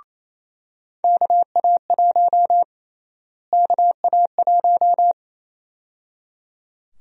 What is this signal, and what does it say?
Morse code, heard as a keyed tone, about 28 wpm, 700 Hz. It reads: KA1 KA1
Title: USA Call Sign Prefixes - Repeated 2x - 28wpm